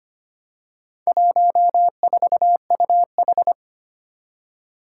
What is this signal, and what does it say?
Morse code, heard as a keyed tone, about 25 wpm, 700 Hz. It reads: 14UH
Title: Four Challenging Letter-Number Combinations 25wpm